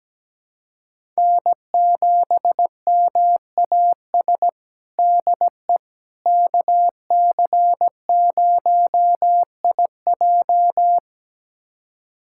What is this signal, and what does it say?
Morse code, heard as a keyed tone, about 17 wpm, 700 Hz. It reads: N7MAS DE KC0IJ
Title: USA Callsign DE Callsign - 5 Characters - 17wpm